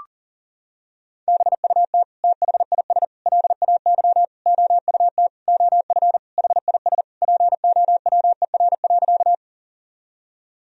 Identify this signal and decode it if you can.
Morse code, heard as a keyed tone, about 40 wpm, 700 Hz. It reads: BUT THIS LAY OUT OF HIS POWER.
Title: Sentences from Top 1000 Words - Review Part 2 - 40wpm